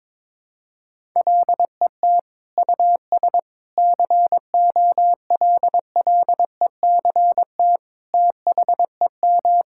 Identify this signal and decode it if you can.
Morse code, heard as a keyed tone, about 22 wpm, 700 Hz. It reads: LET US COLLECT THEM
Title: Sentences from Top 1600 Words 22wpm